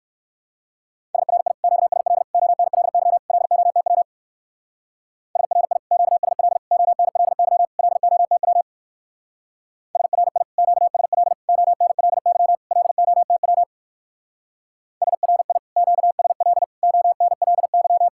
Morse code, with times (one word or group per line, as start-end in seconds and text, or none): 1.1-1.5 SRI
1.6-2.2 QSL
2.3-3.2 ONLY
3.3-4.0 LOTW
5.4-5.8 SRI
5.9-6.6 QSL
6.7-7.7 ONLY
7.8-8.6 LOTW
10.0-10.4 SRI
10.6-11.3 QSL
11.5-12.6 ONLY
12.7-13.6 LOTW
15.0-15.6 SRI
15.8-16.6 QSL
16.8-18.1 ONLY